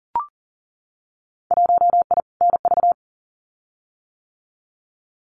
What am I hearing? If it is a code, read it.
Morse code, heard as a keyed tone, about 40 wpm, 700 Hz. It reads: 1I NV
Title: Winter Field Day Exchanges Encore 40wpm